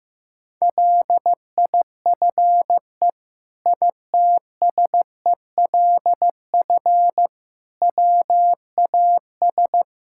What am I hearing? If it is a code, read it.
Morse code, heard as a keyed tone, about 15 wpm, 700 Hz. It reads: LIFE ITSELF WAS
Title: Sentences from Top 1000 Words - Review Part 5 - 15wpm